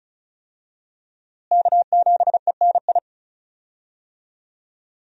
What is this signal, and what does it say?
Morse code, heard as a keyed tone, about 36 wpm, 700 Hz. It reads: K7ENI
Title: USA Call Signs - 5 Characters - Repeated 2x - 35wpm